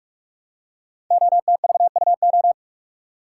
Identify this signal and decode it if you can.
Morse code, heard as a keyed tone, about 44 wpm, 700 Hz. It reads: OTVUO
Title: Code Groups - 5 Letter:Numbers - 45wpm